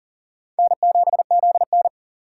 Morse code, heard as a keyed tone, about 39 wpm, 700 Hz. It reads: N7ZN
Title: K1USN SST QSO Practice - 40wpm